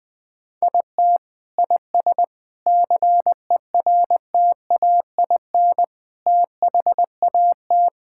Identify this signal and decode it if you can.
Morse code, heard as a keyed tone, about 21 wpm, 700 Hz. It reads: IT IS CERTAIN THAT
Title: Sentences from Top 1000 Words - Review Part 5 - 20wpm